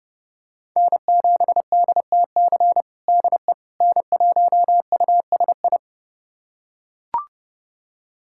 Morse code, heard as a keyed tone, about 30 wpm, 700 Hz. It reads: N7DTC DE N1USI
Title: USA Callsign DE Callsign - 5 Characters - 30wpm